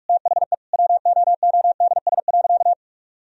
Morse code, heard as a keyed tone, about 46 wpm, 700 Hz. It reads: THE WOODS.
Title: Sentences from Top 1000 Words - Review Part 2 - 45wpm